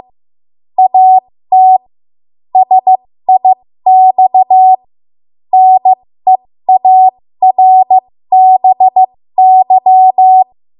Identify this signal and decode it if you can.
Morse code, heard as a keyed tone, about 15 wpm, spaced as for 13 wpm, 750 Hz. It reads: AT SIX NEARBY
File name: Morse / 13WPM